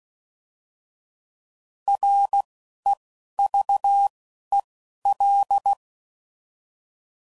Morse code, heard as a keyed tone, about 16 wpm, 800 Hz. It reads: R E V E L
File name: Prefixes / By First Letter / Rs